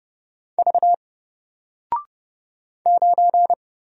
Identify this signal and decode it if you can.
Morse code, heard as a keyed tone, about 30 wpm, 700 Hz. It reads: V 9
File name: Single-Letter-Number-30wpm